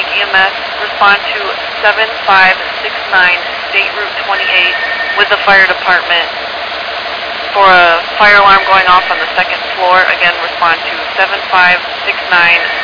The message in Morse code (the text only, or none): none